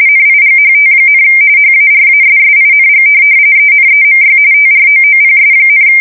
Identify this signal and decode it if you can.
RTTY, about 45.45 baud. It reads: NIGHT, HARASSING ENEMY TROOPS,
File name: fsk170 / USS-New-Jersey